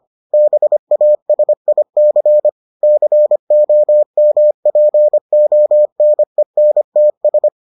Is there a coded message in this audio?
Yes: Morse code from 0.3 to 7.5 s, heard as a keyed tone, about 25 wpm, 600 Hz. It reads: BASIC COMPONENTS